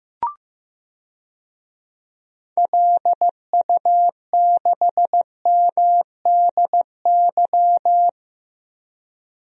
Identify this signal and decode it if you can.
Morse code, heard as a keyed tone, about 15 wpm, 700 Hz. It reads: LU6MDY
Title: DXCC-100 Call Signs - 6 Characters - Repeated 3x - 15wpm